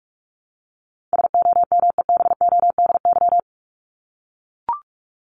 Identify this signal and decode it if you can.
Morse code, heard as a keyed tone, about 44 wpm, 700 Hz. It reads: SOMEBODY
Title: Top 1751-2000 Words 45wpm